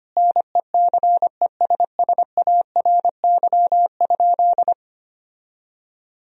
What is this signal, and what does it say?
Morse code, heard as a keyed tone, about 25 wpm, 700 Hz. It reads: NECESSARY?
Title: Sentences from Top 1000 Words - Review Part 3 - 25wpm